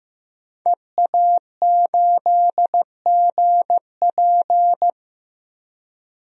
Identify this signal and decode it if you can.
Morse code, heard as a keyed tone, about 15 wpm, 700 Hz. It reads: EA8GP
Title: DXCC-100 Call Signs - 5 Characters - Repeated 2x - 15wpm